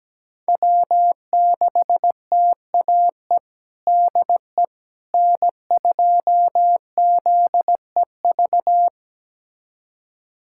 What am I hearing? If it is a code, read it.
Morse code, heard as a keyed tone, about 17 wpm, 700 Hz. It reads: W6TAE DE N2ZEV
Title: USA Callsign DE Callsign - 5 Characters - 17wpm